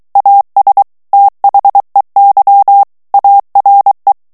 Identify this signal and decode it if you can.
Morse code, heard as a keyed tone, about 23 wpm, 800 Hz. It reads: AS THEY ARE
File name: morse mp3 / ANGLAIS texte clair 1 vitesse 22 mots minute